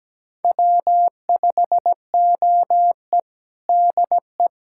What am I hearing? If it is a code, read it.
Morse code, heard as a keyed tone, about 17 wpm, 700 Hz. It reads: W5OE DE
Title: USA Callsign DE Callsign - 4 Characters - 17wpm